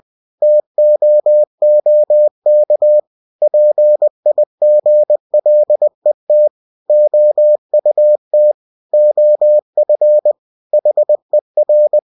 Morse code, heard as a keyed tone, about 20 wpm, 600 Hz. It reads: TOOK PIGLET OUT OF HER
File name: cw-books / winnie-the-pooh / winnie-20wpm-page-0071